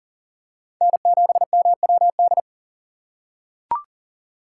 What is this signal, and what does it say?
Morse code, heard as a keyed tone, about 40 wpm, 700 Hz. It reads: N7MWD
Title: USA Call Signs - 6x Character Spacing - 40wpm